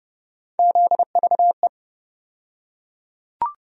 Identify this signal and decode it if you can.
Morse code, heard as a keyed tone, about 30 wpm, 700 Hz. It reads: ZVE
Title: USA Call Sign Postfixes - 3 Letters 30wpm